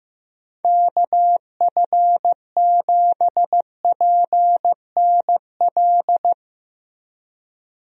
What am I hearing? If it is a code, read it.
Morse code, heard as a keyed tone, about 15 wpm, 700 Hz. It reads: KF7PNL
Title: USA Call Signs - 6 Characters - Repeated 2x - 15wpm